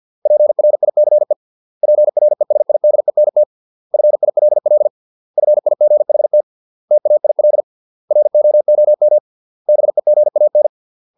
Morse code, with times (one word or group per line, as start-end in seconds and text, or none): 0.3-1.3 PRICE
1.8-3.4 PRESIDENT
3.9-4.9 FILL
5.4-6.4 FIGHT
6.9-7.6 TAIL
8.1-9.2 ROOM
9.7-10.7 BEGAN